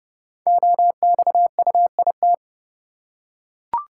Morse code, heard as a keyed tone, about 30 wpm, 700 Hz. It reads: OXUIT
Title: Code Groups - 5 Letters - 30wpm